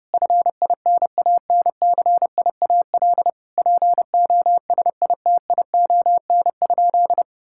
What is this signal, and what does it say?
Morse code, heard as a keyed tone, about 32 wpm, 700 Hz. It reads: FINANCIAL POSITION?
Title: Sentences from Top 1000 Words - Review Part 5 - 30wpm